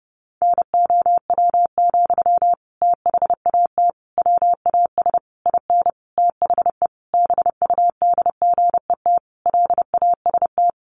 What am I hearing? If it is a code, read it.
Morse code, heard as a keyed tone, about 30 wpm, 700 Hz. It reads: NOW, THAT WAS IN THE BUDGET LAST
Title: Sentences from Top 1200 Words 30wpm